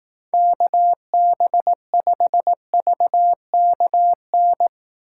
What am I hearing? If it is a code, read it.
Morse code, heard as a keyed tone, about 18 wpm, 700 Hz. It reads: KB5VKN
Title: Speed Racing - USA Call Signs 15wpm